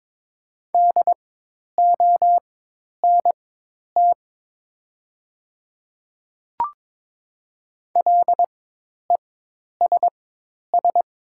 Morse code, heard as a keyed tone, about 23 wpm, spaced as for 10 wpm, 700 Hz. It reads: DONT LESS
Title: 4x Character Spacing - Top 500 Words - 22wpm